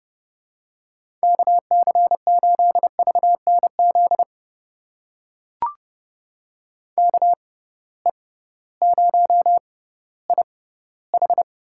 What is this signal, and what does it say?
Morse code, heard as a keyed tone, about 32 wpm, spaced as for 16 wpm, 700 Hz. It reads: KC8VNZ KE0IH
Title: USA Call Signs - 6 Characters - 6x Character Spacing - 30wpm